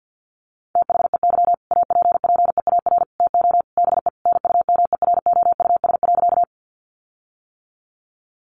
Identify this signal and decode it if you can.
Morse code, heard as a keyed tone, about 50 wpm, 700 Hz. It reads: THEY APPEAR TO BE NUMEROUS.
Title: Sentences from Top 1400 Words 50wpm